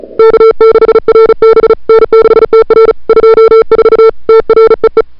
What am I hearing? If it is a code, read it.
Morse code, heard as a keyed tone, about 34 wpm, 440 Hz. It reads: K6RB N6TR 24 TREE
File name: CWSprint / February 2007 CW Sprint / 01-2007 CW Sprint Hour1